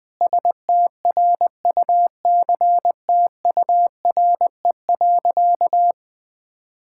Morse code, heard as a keyed tone, about 20 wpm, 700 Hz. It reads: STRUCTURE.
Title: Sentences from Top 1000 Words - Review Part 6 - 20wpm